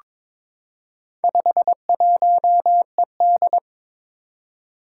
Morse code, heard as a keyed tone, about 23 wpm, 700 Hz. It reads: 51ED
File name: practice / 22wpm / Four Letter-Number Combinations - 22wpm